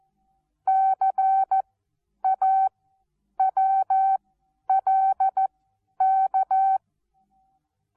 Morse code, heard as a keyed tone, about 14 wpm, spaced as for 9 wpm, 770 Hz. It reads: CAWLK
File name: Onlinelehrgang / morsen / DJ4UF Morsen19